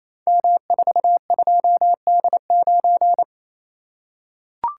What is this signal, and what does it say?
Morse code, heard as a keyed tone, about 27 wpm, 700 Hz. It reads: M42D9
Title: Code Groups - 5 Letter:Numbers - 28wpm